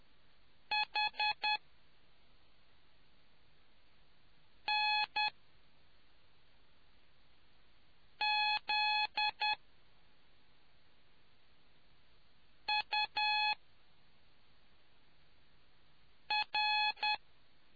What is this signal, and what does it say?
Morse code, heard as a keyed tone, about 10 wpm, spaced as for 3 wpm, 850 Hz. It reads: HNZUR